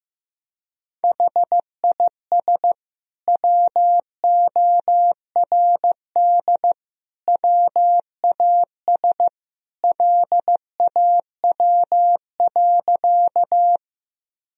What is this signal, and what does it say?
Morse code, heard as a keyed tone, about 15 wpm, 700 Hz. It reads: HIS WORD WAS LAW.
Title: Sentences from Top 1000 Words - Review Part 3 - 15wpm